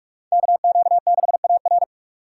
Morse code, heard as a keyed tone, about 44 wpm, 690 Hz. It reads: KQ6AR